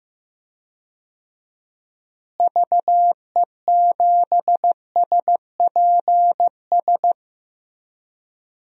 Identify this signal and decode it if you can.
Morse code, heard as a keyed tone, about 15 wpm, 700 Hz. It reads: VE7SPS
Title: DXCC-100 Call Signs - 6 Characters - Repeated 2x - 15wpm